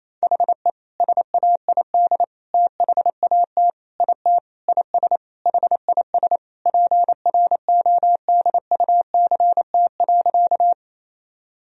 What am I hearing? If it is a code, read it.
Morse code, heard as a keyed tone, about 28 wpm, 700 Hz. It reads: HE SAID THAT IT IS HIS PRODUCT.
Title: Sentences from Top 1000 Words - Review Part 2 - 28wpm